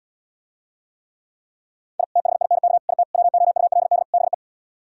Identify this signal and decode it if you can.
Morse code, heard as a keyed tone, about 77 wpm, 690 Hz. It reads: I THINK IT WOULD BE